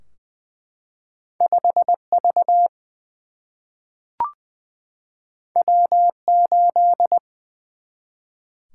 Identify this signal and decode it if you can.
Morse code, heard as a keyed tone, about 20 wpm, 700 Hz. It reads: 5 V W 8
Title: Single-Letter Single-Number 20wpm